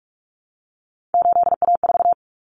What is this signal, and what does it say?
Morse code, heard as a keyed tone, about 44 wpm, 700 Hz. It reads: 8A4
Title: Common Cut Numbers - 3-Digit 45wpm